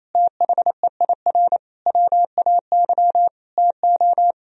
Morse code, heard as a keyed tone, about 28 wpm, 700 Hz. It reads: THEIR WAY TO
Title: Sentences from Top 1000 Words - Review Part 1 - 28wpm